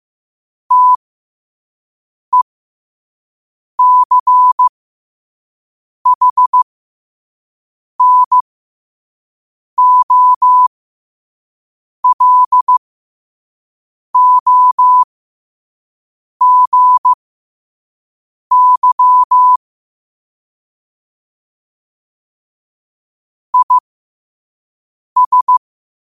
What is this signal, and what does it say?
Morse code, heard as a keyed tone, about 15 wpm, spaced as for 5 wpm, 1000 Hz. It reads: TECHNOLOGY IS
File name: qotdfiles / 5wpm